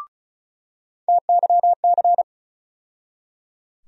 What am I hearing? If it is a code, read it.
Morse code, heard as a keyed tone, about 34 wpm, 700 Hz. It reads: TYC
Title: USA Call Sign Postfixes - 3 Letters 35wpm